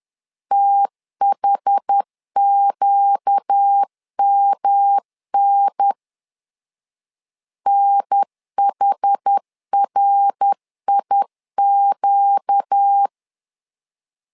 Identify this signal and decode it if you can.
Morse code, heard as a keyed tone, about 10 wpm, 800 Hz. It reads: THQMN NHRIQ
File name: LEZIONI / 4 Q 10wpm 2210